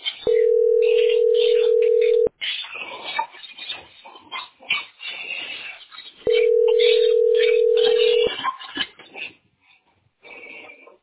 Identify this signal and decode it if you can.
Ringback tone (North America), 440 + 480 Hz.